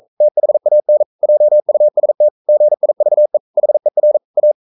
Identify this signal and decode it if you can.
Morse code, heard as a keyed tone, about 42 wpm, 600 Hz. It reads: THAN JUST GIVE HER A
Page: Romanen Never af Ken Follet på engelsk.